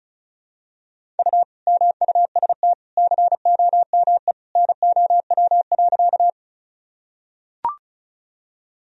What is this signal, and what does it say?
Morse code, heard as a keyed tone, about 34 wpm, 700 Hz. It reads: U MUST COME NOW.
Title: Sentences from Top 200 Words and Abbreviations 35wpm